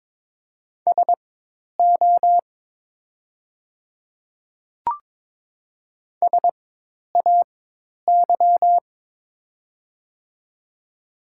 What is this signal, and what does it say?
Morse code, heard as a keyed tone, about 22 wpm, spaced as for 10 wpm, 700 Hz. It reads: SO SAY
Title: 4x Character Spacing - Top 300 Words - 22wpm